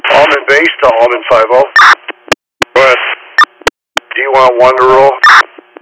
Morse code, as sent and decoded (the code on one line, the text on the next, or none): none